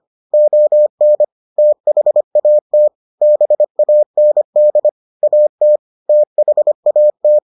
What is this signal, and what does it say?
Morse code, heard as a keyed tone, about 25 wpm, 600 Hz. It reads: ON THAT BAND AT THAT